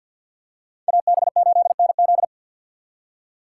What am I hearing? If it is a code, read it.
Morse code, heard as a keyed tone, about 52 wpm, 700 Hz. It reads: AB8NZ
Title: USA Call Signs - 5 Characters - 6x Character Spacing - 50wpm